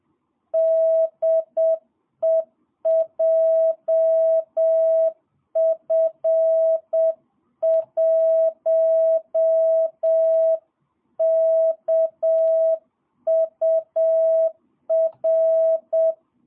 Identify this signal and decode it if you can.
Morse code, heard as a keyed tone, about 7 wpm, 650 Hz. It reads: DE JF1KUR
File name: jh8cbh / 2022cw